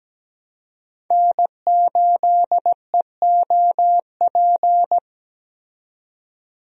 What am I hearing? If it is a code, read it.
Morse code, heard as a keyed tone, about 17 wpm, 700 Hz. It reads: N8EOP
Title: USA Call Signs - 5 Characters - 17wpm